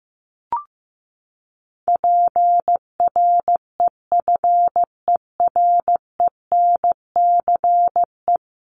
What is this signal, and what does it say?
Morse code, heard as a keyed tone, about 15 wpm, 700 Hz. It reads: PREFERENCE
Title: Top 1251-1500 Words 15wpm